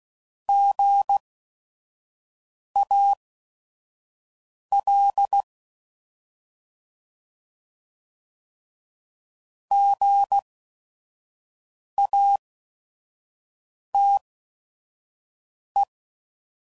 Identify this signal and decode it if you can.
Morse code, heard as a keyed tone, about 16 wpm, spaced as for 5 wpm, 800 Hz. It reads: GAL GATE